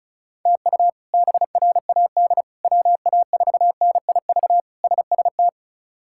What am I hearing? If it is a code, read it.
Morse code, heard as a keyed tone, about 34 wpm, 700 Hz. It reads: TU BRAD WA4NIV SST
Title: K1USN SST QSO Practice - 35wpm